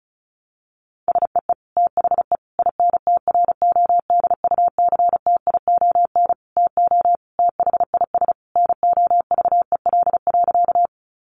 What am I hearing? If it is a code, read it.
Morse code, heard as a keyed tone, about 34 wpm, 700 Hz. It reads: SEE THE INTRODUCTION TO THIS NOVEL.
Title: Sentences from Top 1300 Words 35wpm